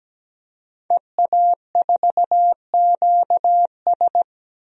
Morse code, heard as a keyed tone, about 17 wpm, 700 Hz. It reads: EA4QS
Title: DXCC-100 Call Signs - 5 Characters - 17wpm